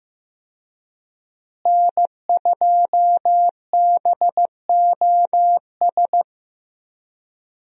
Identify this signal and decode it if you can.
Morse code, heard as a keyed tone, about 15 wpm, 700 Hz. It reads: N2BOS
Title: USA Call Signs - 5 Characters - Repeated 3x - 15wpm